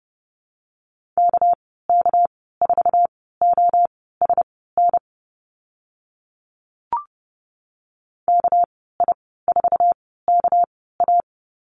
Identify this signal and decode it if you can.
Morse code, heard as a keyed tone, about 30 wpm, spaced as for 17 wpm, 700 Hz. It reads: KK4OSN KI4KA
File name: Lesson 37g - Letter x - Call Signs 30wpm